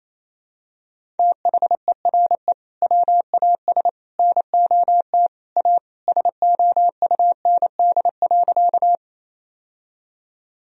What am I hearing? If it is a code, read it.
Morse code, heard as a keyed tone, about 28 wpm, 700 Hz. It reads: THERE WAS NOT A SOUND.
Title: Sentences from Top 1000 Words - Review Part 1 - 28wpm